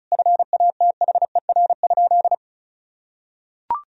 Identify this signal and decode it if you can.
Morse code, heard as a keyed tone, about 36 wpm, 700 Hz. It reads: FATHER?
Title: Sentences from Top 400 Words and Abbreviations 35wpm